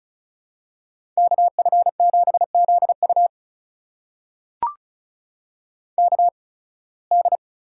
Morse code, heard as a keyed tone, about 36 wpm, spaced as for 23 wpm, 700 Hz. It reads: KF7ZU KD
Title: USA Call Signs - 5 Characters - 8x Character Spacing - 35wpm